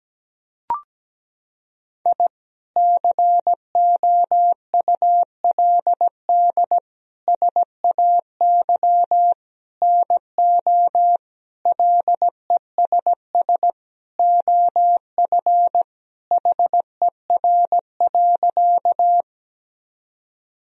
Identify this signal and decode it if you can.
Morse code, heard as a keyed tone, about 17 wpm, 700 Hz. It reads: I COULD SAY NO LESS OF HER.
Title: Sentences from Top 1000 Words - Review Part 2 - 17wpm